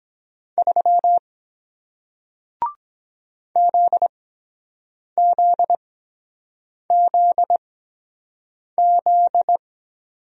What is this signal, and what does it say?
Morse code, heard as a keyed tone, about 23 wpm, spaced as for 7 wpm, 700 Hz. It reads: 3 ZZZZ